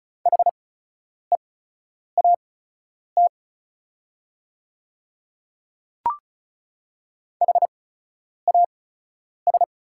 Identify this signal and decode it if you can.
Morse code, heard as a keyed tone, about 34 wpm, spaced as for 10 wpm, 690 Hz. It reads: HEAT HAS
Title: 8x Character Spacing - Top 500 Words - 35wpm